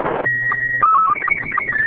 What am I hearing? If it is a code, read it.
SSTV (slow-scan television), Robot 36.